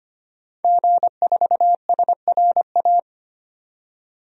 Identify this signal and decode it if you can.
Morse code, heard as a keyed tone, about 25 wpm, 700 Hz. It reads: G4SRA
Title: DXCC-100 Call Signs - 5 Characters - 25wpm